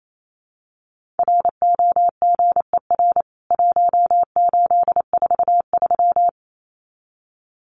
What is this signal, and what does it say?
Morse code, heard as a keyed tone, about 28 wpm, 700 Hz. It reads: ROGER 1843
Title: CWT Exchange 28wpm